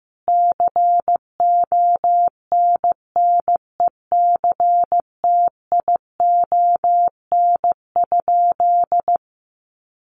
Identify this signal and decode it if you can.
Morse code, heard as a keyed tone, about 15 wpm, 700 Hz. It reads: CONNECTION?
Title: Sentences from Top 1300 Words 15wpm